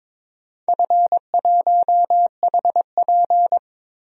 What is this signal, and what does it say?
Morse code, heard as a keyed tone, about 23 wpm, 700 Hz. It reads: F1HP
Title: DXCC-100 Call Signs - 4 Characters - 22wpm